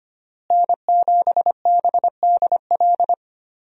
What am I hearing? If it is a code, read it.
Morse code, heard as a keyed tone, about 25 wpm, 700 Hz. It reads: N7BDL
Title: USA Call Signs - 5 Characters - Repeated 3x - 25wpm